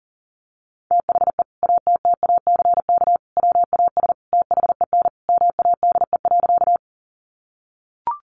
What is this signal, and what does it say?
Morse code, heard as a keyed tone, about 37 wpm, 700 Hz. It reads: THE ATTACK WAS THEN MADE.
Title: Sentences from Top 1500 Words 40wpm